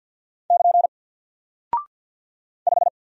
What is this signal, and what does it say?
Morse code, heard as a keyed tone, about 48 wpm, 700 Hz. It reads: <KN> 5
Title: Single Character (Letter:Number:Punctuation:Prosign) 50wpm